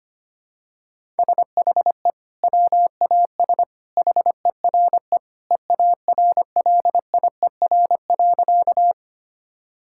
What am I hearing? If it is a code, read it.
Morse code, heard as a keyed tone, about 25 wpm, 700 Hz. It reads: SHE WAS HERE EARLIER.
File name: practice / 25wpm / Sentences with 2-Word Phrases - 25wpm